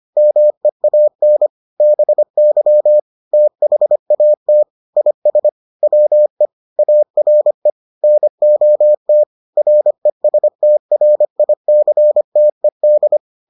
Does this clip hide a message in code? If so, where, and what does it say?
Yes: Morse code from 0.2 to 13.2 s, heard as a keyed tone, about 25 wpm, 600 Hz. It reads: MEAN BY THAT IS WE ARE NOT RESTRICTED